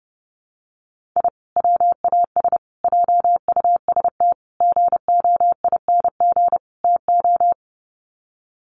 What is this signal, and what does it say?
Morse code, heard as a keyed tone, about 30 wpm, 700 Hz. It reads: I WAS JUST GOING TO
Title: Sets of 5 Words Encore 30wpm